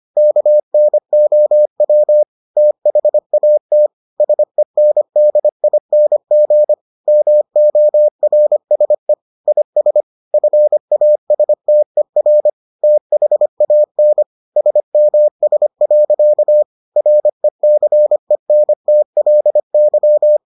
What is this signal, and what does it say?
Morse code, heard as a keyed tone, about 25 wpm, 600 Hz. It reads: KNOW THAT SENDING MORSE IS FASTER THAN SMS. RECENTLY